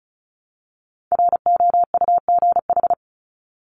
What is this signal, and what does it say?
Morse code, heard as a keyed tone, about 34 wpm, 700 Hz. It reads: ROUGH
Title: Top 1501-1750 Words 35wpm